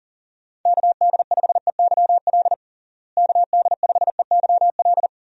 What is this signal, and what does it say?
Morse code, heard as a keyed tone, about 40 wpm, 700 Hz. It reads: KD5EYL KD5EYL
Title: USA Call Signs - 6 Characters - Repeated 3x - 40wpm